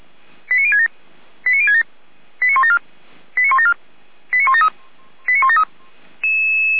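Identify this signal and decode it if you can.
Five-tone selective call (ZVEI), heard as quick runs of five tones, tones of about 70 ms. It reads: 89067 89067 89174 89174 89173
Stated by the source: Five-tone ZVEI selective call 89246, received on SDR.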